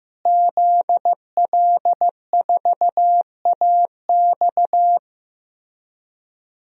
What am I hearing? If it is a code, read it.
Morse code, heard as a keyed tone, about 15 wpm, 700 Hz. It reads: ZL4AX
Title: DXCC-100 Call Signs - 5 Characters - Repeated 3x - 15wpm